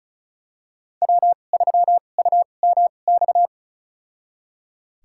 Morse code, heard as a keyed tone, about 36 wpm, spaced as for 25 wpm, 700 Hz. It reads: W3UMX
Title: USA Call Signs - 5 Characters - 2x Character Spacing - 35wpm